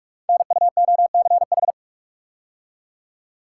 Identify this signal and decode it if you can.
Morse code, heard as a keyed tone, about 44 wpm, 700 Hz. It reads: NUOCH
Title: Code Groups - 5 Letters - 45wpm